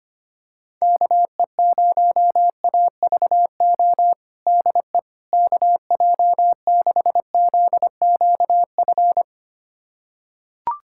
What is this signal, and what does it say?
Morse code, heard as a keyed tone, about 25 wpm, 700 Hz. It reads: KE0AVO DE KJ6ZQF
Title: USA Callsign DE Callsign 25wpm